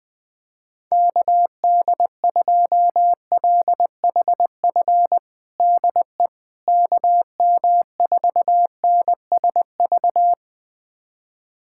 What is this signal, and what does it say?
Morse code, heard as a keyed tone, about 21 wpm, 700 Hz. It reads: KD2LHF DE KM4NSV
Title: USA Callsign DE Callsign - 6 Characters - 20wpm